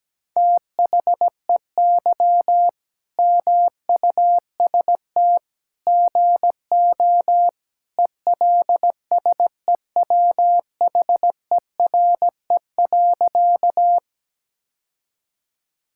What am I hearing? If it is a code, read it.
Morse code, heard as a keyed tone, about 17 wpm, 700 Hz. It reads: THEY MUST GO ELSEWHERE.
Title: Sentences from Top 1600 Words 17wpm